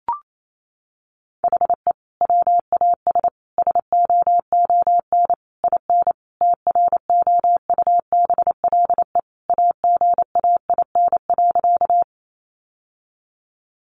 Morse code, heard as a keyed tone, about 28 wpm, 700 Hz. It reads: HE WAS SOON IN TROUBLE AGAIN.
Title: Sentences from Top 1200 Words 28wpm